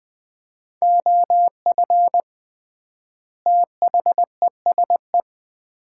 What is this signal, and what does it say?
Morse code, heard as a keyed tone, about 20 wpm, 700 Hz. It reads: OF THESE
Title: 3x Word-Spacing - Sets of 2 Words - 20wpm